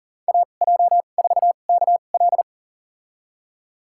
Morse code, heard as a keyed tone, about 40 wpm, spaced as for 29 wpm, 700 Hz. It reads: AJ4KL
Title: USA Call Signs - 5 Characters - 2x Character Spacing - 40wpm